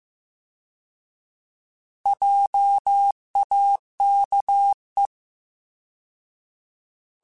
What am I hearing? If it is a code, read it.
Morse code, heard as a keyed tone, about 15 wpm, 800 Hz. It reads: JAKE